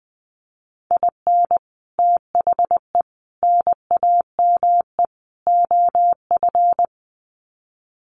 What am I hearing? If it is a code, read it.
Morse code, heard as a keyed tone, about 20 wpm, 700 Hz. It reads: IN THE NAME OF
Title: Sets of 4 Words 20wpm